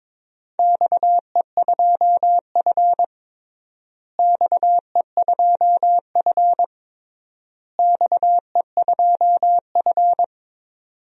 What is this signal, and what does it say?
Morse code, heard as a keyed tone, about 23 wpm, 700 Hz. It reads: XE2F XE2F XE2F
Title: DXCC-100 Call Signs - 4 Characters - Repeated 3x - 22wpm